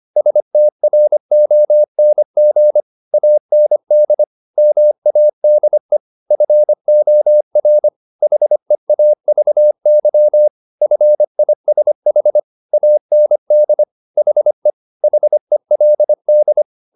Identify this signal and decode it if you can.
Morse code, heard as a keyed tone, about 25 wpm, 600 Hz. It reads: STRONG AND MADE FOR HEAVY FISH AND HE HELD
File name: the old man and the sea part 1 25wpm